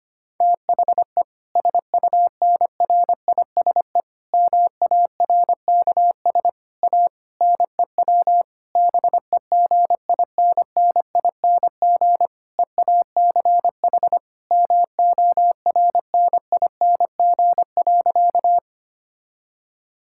Morse code, heard as a keyed tone, about 25 wpm, 700 Hz. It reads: THE SUNRISE MARKS A NEW BEGINNING EACH MORNING.
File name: practice / 25wpm / Difficult Sentences with Hints - 25wpm